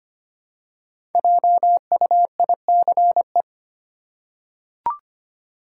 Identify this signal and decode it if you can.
Morse code, heard as a keyed tone, about 25 wpm, 700 Hz. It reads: JUICE
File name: 5 Letter Words - First 3 Letters Given - 25wpm